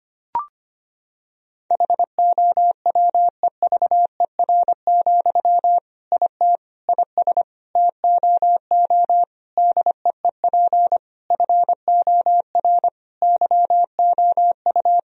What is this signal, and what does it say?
Morse code, heard as a keyed tone, about 25 wpm, 700 Hz. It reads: HOWEVER, IT IS TOO DEEP FOR YOU
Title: Sentences from Top 1000 Words - Review Part 3 - 25wpm